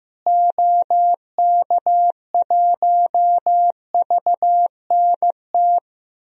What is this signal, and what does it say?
Morse code, heard as a keyed tone, about 16 wpm, 700 Hz. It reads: OK1VNT
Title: DXCC-100 Call Signs - 6 Characters - Repeated 2x - 15wpm